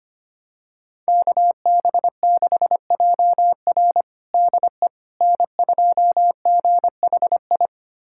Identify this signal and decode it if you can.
Morse code, heard as a keyed tone, about 25 wpm, 700 Hz. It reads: KB6JR DE N2GHI
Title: USA Callsign DE Callsign 25wpm